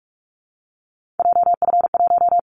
Morse code, heard as a keyed tone, about 44 wpm, 700 Hz. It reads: JF1
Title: Callsign Prefixes for Top 25 DXCC - Repeated 3x - 45wpm